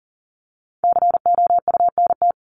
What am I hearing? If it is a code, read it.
Morse code, heard as a keyed tone, about 40 wpm, 700 Hz. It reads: COUNT
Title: Top 1251-1500 Words 40wpm